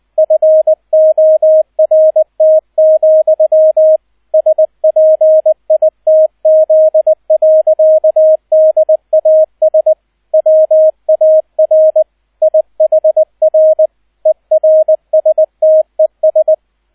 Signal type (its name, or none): Morse code